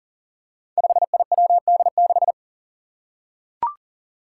Morse code, heard as a keyed tone, about 42 wpm, 700 Hz. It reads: 5IWD6
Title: Five Challenging Letter-Number Combinations 40wpm